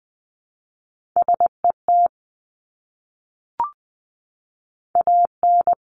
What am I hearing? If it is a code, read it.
Morse code, heard as a keyed tone, about 20 wpm, 700 Hz. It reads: SET AN
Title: Top 200 Words Repeated 20wpm